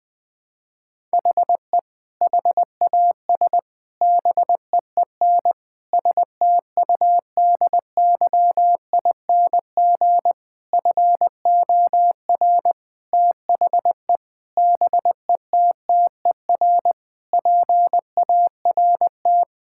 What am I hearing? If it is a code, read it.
Morse code, heard as a keyed tone, about 21 wpm, 700 Hz. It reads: HE HAS BEEN STUDYING FOR THE BETTER PART
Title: Sentences with 7-Word Phrases - 20wpm